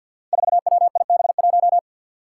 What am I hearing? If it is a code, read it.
Morse code, heard as a keyed tone, about 53 wpm, 700 Hz. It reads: 4WIB1
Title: Five Challenging Letter-Number Combinations 50wpm